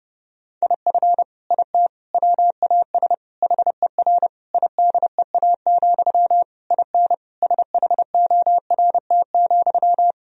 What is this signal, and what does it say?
Morse code, heard as a keyed tone, about 32 wpm, 700 Hz. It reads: IF IT WAS HER IDEA, IN SHORT,
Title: Sentences from Top 1000 Words - Review Part 2 - 30wpm